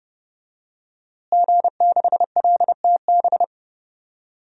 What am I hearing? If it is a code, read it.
Morse code, heard as a keyed tone, about 31 wpm, 700 Hz. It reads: G6LTB
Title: DXCC-100 Call Signs - 5 Characters - Repeated 2x - 30wpm